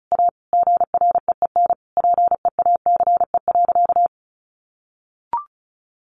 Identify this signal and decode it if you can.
Morse code, heard as a keyed tone, about 34 wpm, 700 Hz. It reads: A GREEN PEACE.
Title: A Sound Of Thunder 35wpm